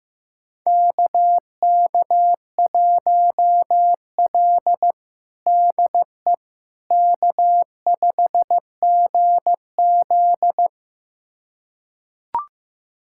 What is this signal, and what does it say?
Morse code, heard as a keyed tone, about 16 wpm, 700 Hz. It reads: KK1L DE K5GZ
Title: USA Callsign DE Callsign - 4 Characters - 15wpm